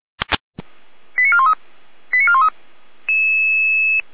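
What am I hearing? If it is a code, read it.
Five-tone selective call (ZVEI), heard as quick runs of five tones, tones of about 70 ms. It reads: 89413 89413
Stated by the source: ZVEI five-tone selective call received on an SDR receiver.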